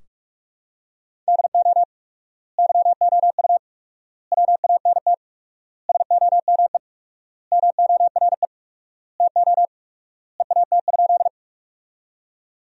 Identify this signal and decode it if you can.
Morse code, heard as a keyed tone, about 43 wpm, 700 Hz. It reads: DO YOU WANT SOME MORE TO EAT?
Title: 4x Word-Spacing - Sentences from Top 300 Words - 45wpm